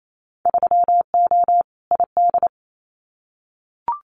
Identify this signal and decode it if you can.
Morse code, heard as a keyed tone, about 28 wpm, 700 Hz. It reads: 3O ID
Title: Winter Field Day Exchanges 28wpm